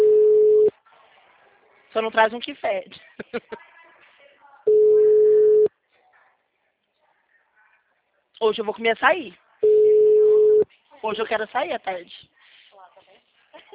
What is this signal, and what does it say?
Ringback tone (Europe, 425 Hz), with speech.